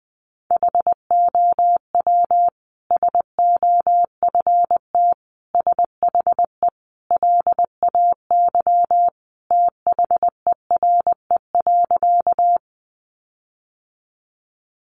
Morse code, heard as a keyed tone, about 20 wpm, 700 Hz. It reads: HOW SOFT SHE LAY THERE.
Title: Sentences from Top 1500 Words 20wpm